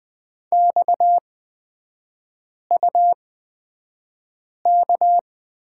Morse code, heard as a keyed tone, about 20 wpm, spaced as for 5 wpm, 690 Hz. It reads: XUK
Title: 229 Lesson 37e - Letter x - Three Characters 20wpm